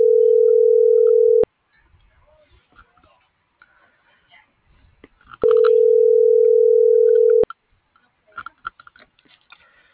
Ringback tone (North America), 440 + 480 Hz.